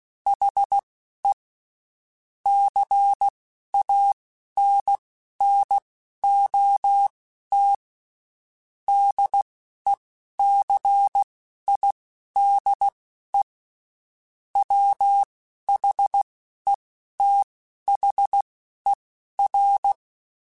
Morse code, heard as a keyed tone, about 16 wpm, spaced as for 11 wpm, 800 Hz. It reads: HE CANNOT DECIDE WHETHER